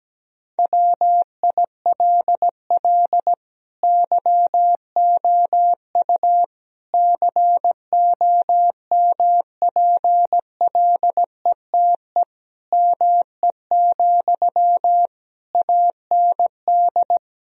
Morse code, heard as a keyed tone, about 17 wpm, 700 Hz. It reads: WILL YOU COMPLETE ME, AND
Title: Sentences from Top 1000 Words - Review Part 2 - 17wpm